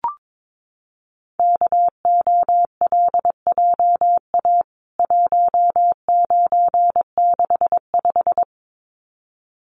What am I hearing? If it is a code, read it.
Morse code, heard as a keyed tone, about 22 wpm, 700 Hz. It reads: KOLJA 1965
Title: CWT Exchange 22wpm